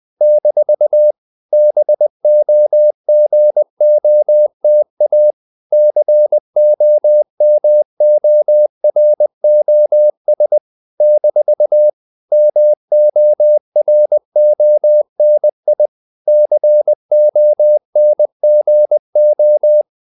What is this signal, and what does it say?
Morse code, heard as a keyed tone, about 20 wpm, 600 Hz. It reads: - BOGOTA COMOROS - MORONI CONGO
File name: CW / Practice / NationalCapitals20